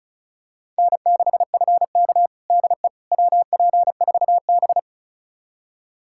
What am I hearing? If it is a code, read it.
Morse code, heard as a keyed tone, about 36 wpm, 700 Hz. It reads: N6FK DE WP4B
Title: USA Callsign DE Callsign - 4 Characters - 35wpm